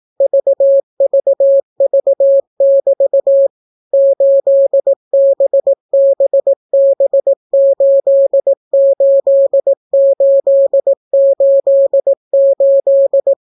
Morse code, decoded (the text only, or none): VVV<BT> 8BBB88888